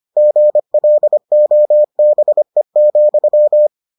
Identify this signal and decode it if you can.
Morse code, heard as a keyed tone, about 25 wpm, 600 Hz. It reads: GLOBE,